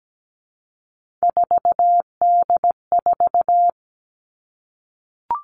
Morse code, heard as a keyed tone, about 17 wpm, 700 Hz. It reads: 4D4
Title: All Cut Numbers - 3-Digit 17wpm